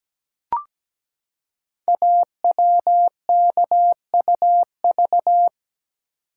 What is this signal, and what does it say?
Morse code, heard as a keyed tone, about 17 wpm, 700 Hz. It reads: AWKUV
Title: Code Groups - 5 Letters - 17wpm